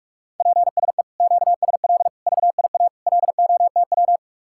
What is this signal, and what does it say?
Morse code, heard as a keyed tone, about 49 wpm, 700 Hz. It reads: PSE QSL VIA LOTW